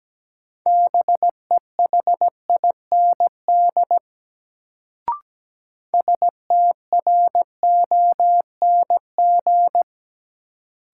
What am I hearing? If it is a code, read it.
Morse code, heard as a keyed tone, about 17 wpm, 700 Hz. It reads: BEHIND STRONG
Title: Sentences from Top 500 Words - New Word Drill - 17wpm